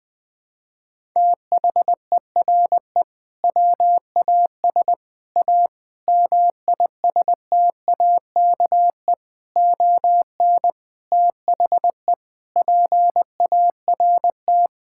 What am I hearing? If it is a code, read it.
Morse code, heard as a keyed tone, about 21 wpm, 700 Hz. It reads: THERE WAS A MISTAKE ON THE PART
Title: Sentences with 5-Word Phrases - 20wpm